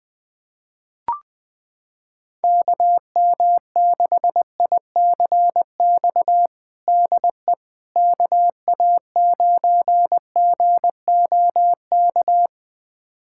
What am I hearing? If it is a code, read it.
Morse code, heard as a keyed tone, about 20 wpm, 700 Hz. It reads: KM6ICX DE KA9GOK
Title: USA Callsign DE Callsign - 6 Characters - 20wpm